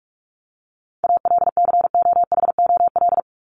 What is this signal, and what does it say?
Morse code, heard as a keyed tone, about 44 wpm, 700 Hz. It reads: ALCOHOL
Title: Top 1251-1500 Words 45wpm